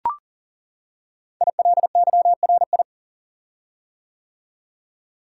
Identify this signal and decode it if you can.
Morse code, heard as a keyed tone, about 39 wpm, 690 Hz. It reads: ILYRI
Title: Code Groups - 5 Letters - 40wpm